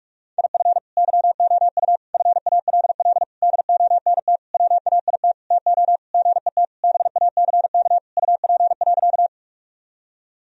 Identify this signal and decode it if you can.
Morse code, heard as a keyed tone, about 41 wpm, 700 Hz. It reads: IF YOU FALL DONT WAIT TO GET BACK UP.
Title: Sentences from Top 1000 Words - Review Part 2 - 45wpm